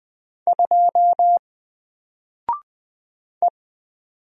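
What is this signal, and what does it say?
Morse code, heard as a keyed tone, about 20 wpm, 700 Hz. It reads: 2 E
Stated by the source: Single letter and number Morse drill at 20 wpm, from N4LCD.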